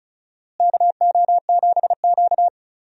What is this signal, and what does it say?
Morse code, heard as a keyed tone, about 36 wpm, 700 Hz. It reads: KO7Q
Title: USA Call Signs - 4 Characters - Repeated 2x - 35wpm